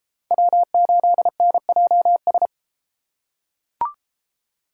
Morse code, heard as a keyed tone, about 29 wpm, 700 Hz. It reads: W8NJS